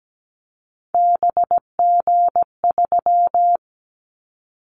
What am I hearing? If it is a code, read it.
Morse code, heard as a keyed tone, about 17 wpm, 700 Hz. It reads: BG3
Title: Callsign Prefixes for Top 100 DXCC - 17wpm